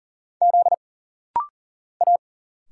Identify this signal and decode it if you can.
Morse code, heard as a keyed tone, about 40 wpm, 700 Hz. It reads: Z A
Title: Single Letter-Number Alt - Rapid-Fire 40wpm